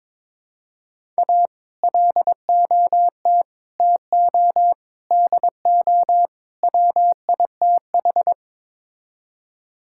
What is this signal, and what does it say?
Morse code, heard as a keyed tone, about 22 wpm, 700 Hz. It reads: A LOT TO DO WITH
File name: practice / 22wpm / Build A Phrase - 5-Word Phrases - 22wpm